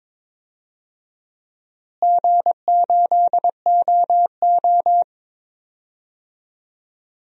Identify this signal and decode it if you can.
Morse code, heard as a keyed tone, about 22 wpm, 700 Hz. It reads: G8OO
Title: DXCC-100 Call Signs - 4 Characters - Repeated 2x - 22wpm